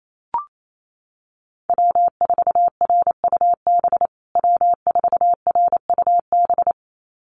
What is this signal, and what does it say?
Morse code, heard as a keyed tone, about 28 wpm, 700 Hz. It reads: W4RUB W4RUB
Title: USA Call Signs Repeated 2x 28wpm